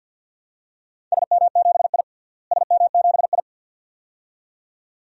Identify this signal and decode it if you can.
Morse code, heard as a keyed tone, about 52 wpm, 700 Hz. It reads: SM7I SM7I
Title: DXCC-100 Call Signs - 4 Characters - Repeated 2x - 50wpm